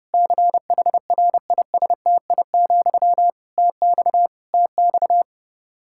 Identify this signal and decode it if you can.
Morse code, heard as a keyed tone, about 30 wpm, 700 Hz. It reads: CHRISTI, TX TX
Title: K1USN SST QSO Practice 30wpm